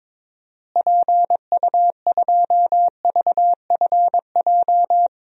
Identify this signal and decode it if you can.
Morse code, heard as a keyed tone, about 23 wpm, 700 Hz. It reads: PU2VFJ
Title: DXCC-100 Call Signs - 6 Characters - Repeated 2x - 22wpm